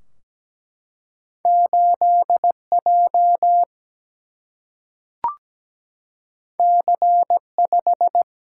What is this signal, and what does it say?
Morse code, heard as a keyed tone, about 17 wpm, 700 Hz. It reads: 8 J C 5
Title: Single-Letter Single-Number 17wpm